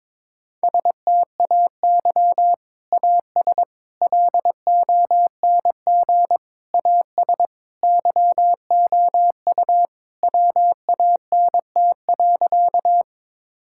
Morse code, heard as a keyed tone, about 22 wpm, 700 Hz. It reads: STAY AS LONG AS YOU WANT.
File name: practice / 22wpm / Sentences with 3-Word Phrases - 22wpm